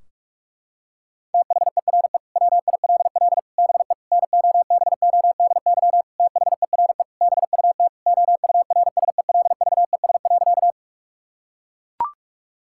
Morse code, heard as a keyed tone, about 41 wpm, 700 Hz. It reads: THERE WILL BE NOBODY THERE BUT OURSELVES.
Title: Sentences from Top 1600 Words 45wpm